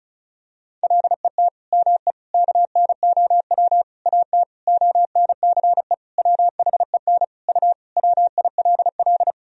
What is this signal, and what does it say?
Morse code, heard as a keyed tone, about 34 wpm, 700 Hz. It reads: LET ME KNOW AT ONCE WHEN U WILL
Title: Sentences from Top 300 Words and Abbreviations 35wpm